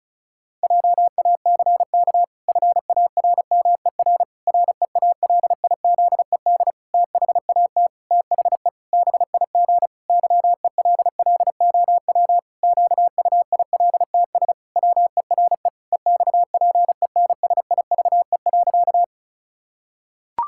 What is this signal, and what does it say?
Morse code, heard as a keyed tone, about 34 wpm, 700 Hz. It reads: JACK FARMER REALIZED THAT THE BIG YELLOW QUILTS WERE EXPENSIVE.
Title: Pangram Challenge v2 - 35wpm